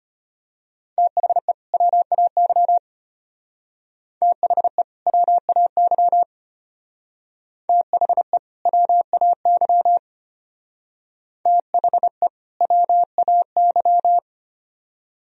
Morse code, heard as a keyed tone, about 32 wpm, 700 Hz. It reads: THE WAY THE WAY THE WAY THE WAY